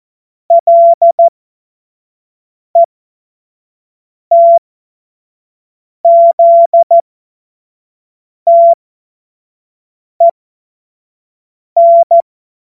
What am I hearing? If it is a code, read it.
Morse code, heard as a keyed tone, about 14 wpm, spaced as for 5 wpm, 680 Hz. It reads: LETZTEN
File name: Morsekurs / Witze / Witz 08. November 2022 25 F70